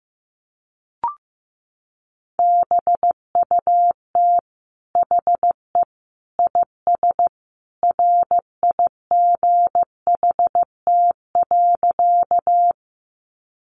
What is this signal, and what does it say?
Morse code, heard as a keyed tone, about 14 wpm, 700 Hz. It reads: BUT HE IS RIGHT.
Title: Sentences from Top 200 Words 15wpm